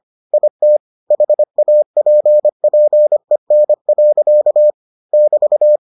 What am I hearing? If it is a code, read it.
Morse code, heard as a keyed tone, about 25 wpm, 600 Hz. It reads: IT HAPPEN. <BT>